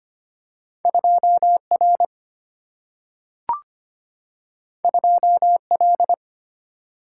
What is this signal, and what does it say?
Morse code, heard as a keyed tone, about 25 wpm, 700 Hz. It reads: 2R 2L
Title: Single-Letter Single-Number 25wpm